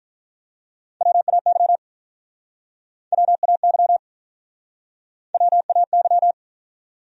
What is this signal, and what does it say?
Morse code, heard as a keyed tone, about 47 wpm, 700 Hz. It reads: WAY WAY WAY